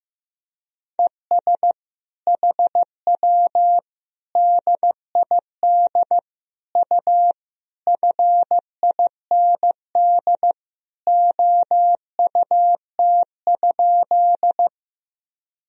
Morse code, heard as a keyed tone, about 15 wpm, 700 Hz. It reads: ES HW DID U FIND OUT?
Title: Sentences from Top 100 Words and Abbreviations 15wpm